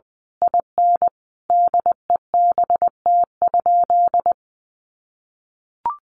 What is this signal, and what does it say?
Morse code, heard as a keyed tone, about 20 wpm, 700 Hz. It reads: IN DEBT?
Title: Sentences from Top 1300 Words 20wpm